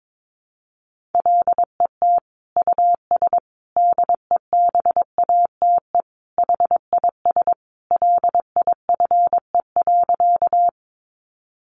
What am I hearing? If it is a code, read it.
Morse code, heard as a keyed tone, about 22 wpm, 700 Hz. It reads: LET US DEBATE HIS LIFE.
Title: Sentences from Top 1100 Words 22wpm